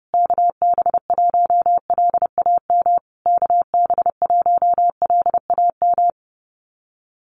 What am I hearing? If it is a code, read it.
Morse code, heard as a keyed tone, about 30 wpm, 700 Hz. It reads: KB1LAM KB1LAM
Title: USA Call Signs Repeated 3x 30wpm